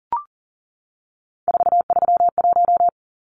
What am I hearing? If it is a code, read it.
Morse code, heard as a keyed tone, about 40 wpm, 700 Hz. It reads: 431
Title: Numbers - 3-Digit 40wpm